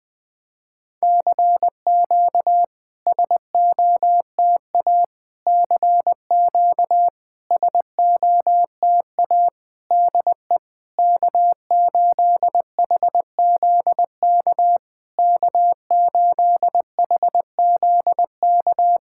Morse code, heard as a keyed tone, about 20 wpm, 690 Hz. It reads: CQ SOTA CQ SOTA DE K8HZK K8HZK
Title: SOTA QSO Practice 20wpm